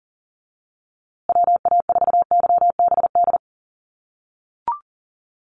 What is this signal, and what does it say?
Morse code, heard as a keyed tone, about 40 wpm, 700 Hz. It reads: WA4YBD
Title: USA Call Signs Repeated 3x 40wpm